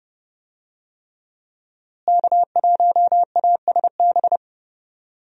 Morse code, heard as a keyed tone, about 30 wpm, 700 Hz. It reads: K1ASB
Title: USA Call Signs - 5 Characters - Repeated 3x - 30wpm